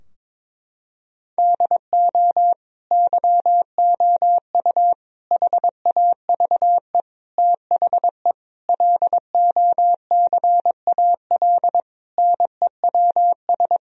Morse code, heard as a keyed tone, about 22 wpm, 700 Hz. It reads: DO YOU HAVE THE LOCAL NEWS
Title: Sentences from Top 1000 Words - Review Part 3 - 22wpm